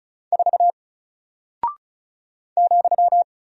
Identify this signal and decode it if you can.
Morse code, heard as a keyed tone, about 35 wpm, 700 Hz. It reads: 4 ,
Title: Single Character (Letter:Number:Punctuation:Prosign) 35wpm